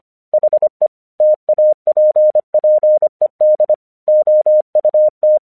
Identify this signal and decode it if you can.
Morse code, heard as a keyed tone, about 25 wpm, 610 Hz. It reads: HE TAPPED OUT
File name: 1 English 25wpm / AQC30